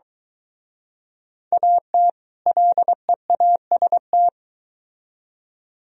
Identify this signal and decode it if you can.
Morse code, heard as a keyed tone, about 23 wpm, 700 Hz. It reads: AT LEAST